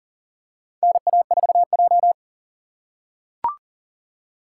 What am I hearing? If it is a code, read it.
Morse code, heard as a keyed tone, about 40 wpm, 700 Hz. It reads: NA4J
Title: USA Call Signs - 4 Characters - 3x Character Spacing - 40wpm